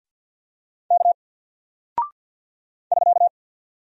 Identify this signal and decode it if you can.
Morse code, heard as a keyed tone, about 48 wpm, 700 Hz. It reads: K <SK>
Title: Single Character (Letter:Number:Punctuation:Prosign) Alt 50wpm